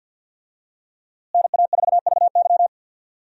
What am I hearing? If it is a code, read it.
Morse code, heard as a keyed tone, about 53 wpm, 700 Hz. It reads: NA4VY
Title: USA Call Signs - 5 Characters - Repeated 2x - 50wpm